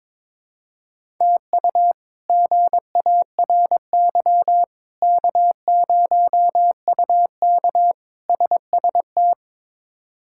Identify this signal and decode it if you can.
Morse code, heard as a keyed tone, about 22 wpm, 700 Hz. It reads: TU GARY K0UK SST
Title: K1USN SST QSO Practice - 22wpm